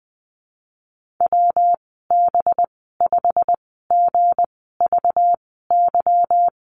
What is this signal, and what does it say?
Morse code, heard as a keyed tone, about 20 wpm, spaced as for 15 wpm, 700 Hz. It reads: WB5GVY
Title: 193 Lesson 31g - Letter v - Call Signs 20wpm